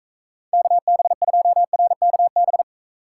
Morse code, heard as a keyed tone, about 44 wpm, 700 Hz. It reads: KB2RKB